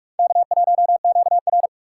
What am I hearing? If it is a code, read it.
Morse code, heard as a keyed tone, about 59 wpm, 700 Hz. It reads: K1QR